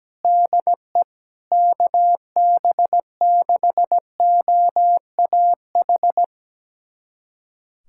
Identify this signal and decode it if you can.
Morse code, heard as a keyed tone, about 17 wpm, 700 Hz. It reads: DE KB6OAH
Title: USA Callsign DE Callsign 17wpm